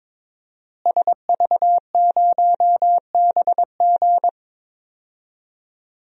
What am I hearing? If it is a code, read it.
Morse code, heard as a keyed tone, about 22 wpm, 700 Hz. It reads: SV0BG
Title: DXCC-100 Call Signs - 5 Characters - Repeated 3x - 22wpm